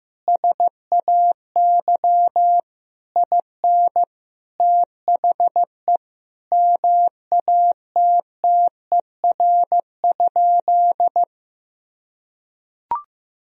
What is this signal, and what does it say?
Morse code, heard as a keyed tone, about 14 wpm, 700 Hz. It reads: SAY IN THE MATTER?
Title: Sentences from Top 1000 Words - Review Part 3 - 15wpm